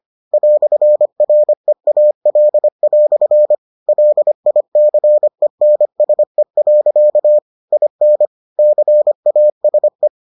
Morse code, heard as a keyed tone, about 25 wpm, 600 Hz. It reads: "REAL" LICENSE. IN CASE